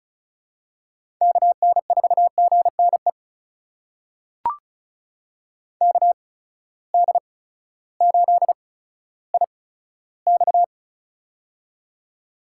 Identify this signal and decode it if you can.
Morse code, heard as a keyed tone, about 36 wpm, spaced as for 16 wpm, 700 Hz. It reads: KN4GNE KD8IX
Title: USA Call Signs - 8x Character Spacing - 35wpm